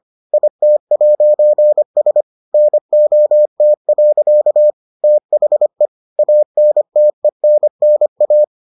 Morse code, heard as a keyed tone, about 25 wpm, 600 Hz. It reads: IT'S NOT. THE ANTENNA